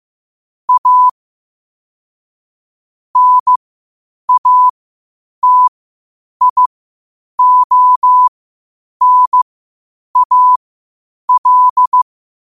Morse code, heard as a keyed tone, about 15 wpm, spaced as for 8 wpm, 1000 Hz. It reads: A NATIONAL